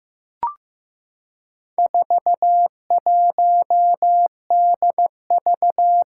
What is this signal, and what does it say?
Morse code, heard as a keyed tone, about 15 wpm, 700 Hz. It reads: 41DV
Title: Four Challenging Letter-Number Combinations 15wpm